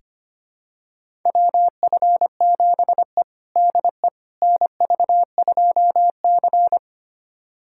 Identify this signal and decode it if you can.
Morse code, heard as a keyed tone, about 25 wpm, 700 Hz. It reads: WF7E DE NV2C
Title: USA Callsign DE Callsign - 4 Characters - 25wpm